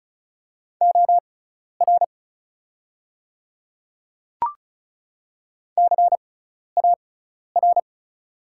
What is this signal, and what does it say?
Morse code, heard as a keyed tone, about 36 wpm, spaced as for 12 wpm, 700 Hz. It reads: OR CAR
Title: 6x Character Spacing - Top 300 Words - 35wpm